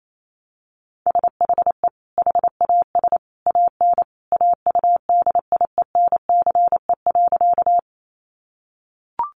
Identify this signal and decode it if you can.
Morse code, heard as a keyed tone, about 28 wpm, 700 Hz. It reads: SHE HAS AN AUDIENCE.
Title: Sentences from Top 800 Words 28wpm